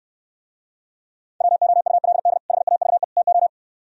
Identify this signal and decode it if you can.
Morse code, heard as a keyed tone, about 67 wpm, 700 Hz. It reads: WOULD HAVE TO